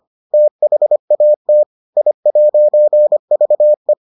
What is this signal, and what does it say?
Morse code, heard as a keyed tone, about 25 wpm, 600 Hz. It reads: THAT I'VE